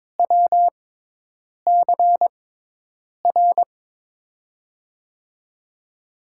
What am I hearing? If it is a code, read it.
Morse code, heard as a keyed tone, about 22 wpm, spaced as for 8 wpm, 700 Hz. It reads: WCR
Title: USA Call Signs - 5 Characters - 6x Character Spacing - 22wpm